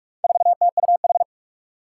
Morse code, heard as a keyed tone, about 46 wpm, 700 Hz. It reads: 4TUH
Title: Four Challenging Letter-Number Combinations 45wpm